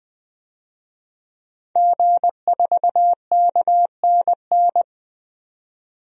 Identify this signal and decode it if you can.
Morse code, heard as a keyed tone, about 20 wpm, 700 Hz. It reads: G4KNN
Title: DXCC-100 Call Signs - 5 Characters - 20wpm